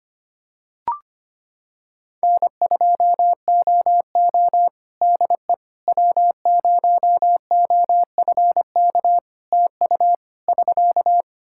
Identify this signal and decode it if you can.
Morse code, heard as a keyed tone, about 25 wpm, 700 Hz. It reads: N2OO DE W0OFK TU <SK>
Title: K1USN SST QSO Practice 25wpm